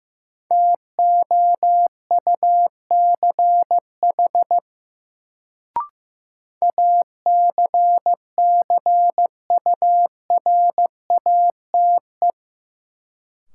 Morse code, heard as a keyed tone, about 15 wpm, 700 Hz. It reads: TOUCH ACCURATE
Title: Sentences from Top 1200 Words - New Word Drill - 15wpm